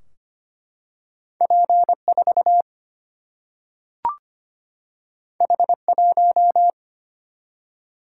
Morse code, heard as a keyed tone, about 25 wpm, 700 Hz. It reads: P 4 H 1
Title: Single-Letter Single-Number 25wpm